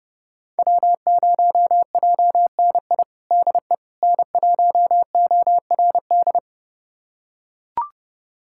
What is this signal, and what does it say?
Morse code, heard as a keyed tone, about 30 wpm, 700 Hz. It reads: W0JNI DE N1ORD
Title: USA Callsign DE Callsign - 5 Characters - 30wpm